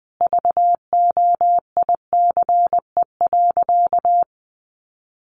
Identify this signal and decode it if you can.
Morse code, heard as a keyed tone, about 19 wpm, 700 Hz. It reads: VOICE.
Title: Sentences from Top 1500 Words 20wpm